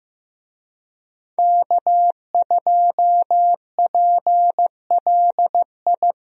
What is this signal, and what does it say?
Morse code, heard as a keyed tone, about 15 wpm, 700 Hz. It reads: K2PLI
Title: USA Call Signs - 5 Characters - 15wpm